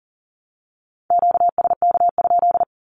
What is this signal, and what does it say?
Morse code, heard as a keyed tone, about 40 wpm, 700 Hz. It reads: QSK?
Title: Q-Codes Repeated 3x 40wpm